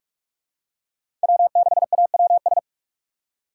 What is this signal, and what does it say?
Morse code, heard as a keyed tone, about 46 wpm, 700 Hz. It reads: W6AWS
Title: K1USN SST QSO Practice - 45wpm